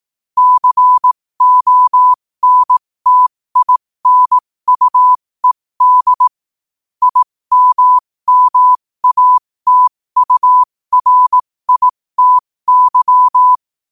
Morse code, heard as a keyed tone, about 18 wpm, spaced as for 15 wpm, 1000 Hz. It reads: CONTINUED IMMATURITY
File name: qotdfiles / 15wpm